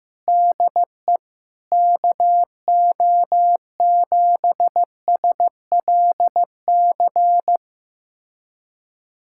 Morse code, heard as a keyed tone, about 15 wpm, 700 Hz. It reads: DE KO7SLC
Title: USA Callsign DE Callsign - 6 Characters - 15wpm